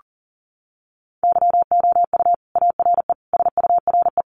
Morse code, heard as a keyed tone, about 40 wpm, 700 Hz. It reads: YOU ARE SURE
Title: Sentences from Top 300 Words 40wpm